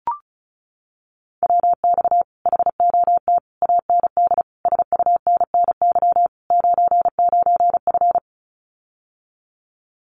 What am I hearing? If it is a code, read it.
Morse code, heard as a keyed tone, about 34 wpm, 700 Hz. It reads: WX HOT AND SUNNY 99F
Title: Common QSO Phrases Encore 35wpm